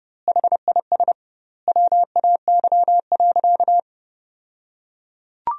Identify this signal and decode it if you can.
Morse code, heard as a keyed tone, about 32 wpm, 700 Hz. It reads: HIS WAY.
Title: 2x Word-Spacing - Sentences from Top 300 Words - 30wpm